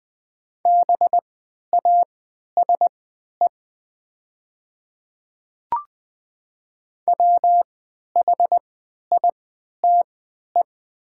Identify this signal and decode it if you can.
Morse code, heard as a keyed tone, about 20 wpm, spaced as for 11 wpm, 700 Hz. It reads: BASE WHITE
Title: 3x Character Spacing - Top 500 Words - 20wpm